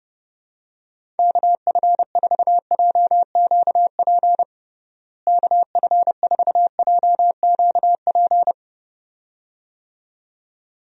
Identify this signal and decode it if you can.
Morse code, heard as a keyed tone, about 30 wpm, 700 Hz. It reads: KF4JQP KF4JQP
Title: USA Call Signs - 6 Characters - Repeated 2x - 30wpm